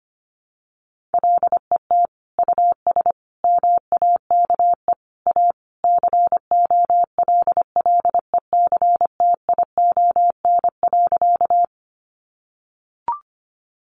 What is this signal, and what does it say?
Morse code, heard as a keyed tone, about 25 wpm, 700 Hz. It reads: LET US MAKE A COLLECTION.
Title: Sentences from Top 1200 Words 25wpm